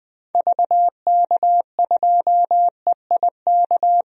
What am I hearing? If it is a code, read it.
Morse code, heard as a keyed tone, about 20 wpm, 700 Hz. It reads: VK2EIK
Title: DXCC-100 Call Signs - 6 Characters - 20wpm